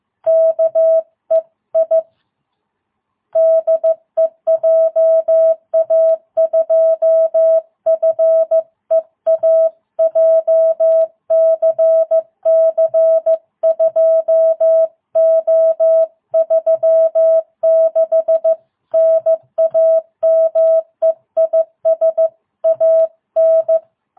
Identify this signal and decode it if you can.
Morse code, heard as a keyed tone, about 15 wpm, 660 Hz. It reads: KEI DE JA2FEA JCC 2O36 NAME IS AN